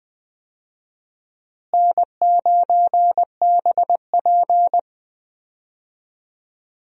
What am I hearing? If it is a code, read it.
Morse code, heard as a keyed tone, about 20 wpm, 700 Hz. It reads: N9BP
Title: USA Call Signs - 4 Characters - Repeated 2x - 20wpm